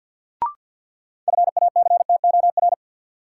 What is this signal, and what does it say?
Morse code, heard as a keyed tone, about 52 wpm, 700 Hz. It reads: FACTOR
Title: Sentences from Top 1300 Words - New Word Drill - 50wpm